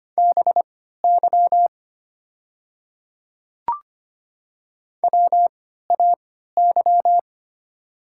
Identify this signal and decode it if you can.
Morse code, heard as a keyed tone, about 25 wpm, spaced as for 14 wpm, 700 Hz. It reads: BY WAY
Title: 3x Character Spacing - Top 200 Words - 25wpm